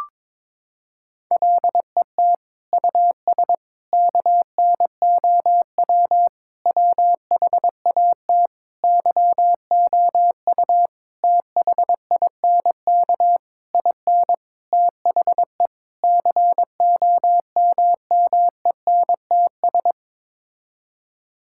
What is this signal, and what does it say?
Morse code, heard as a keyed tone, about 22 wpm, 700 Hz. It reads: LET US KNOW WHAT YOU THINK IN THE COMMENTS
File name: Sets of 9 Words - 22wpm